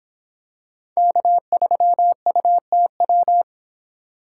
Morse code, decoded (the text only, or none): K3UTW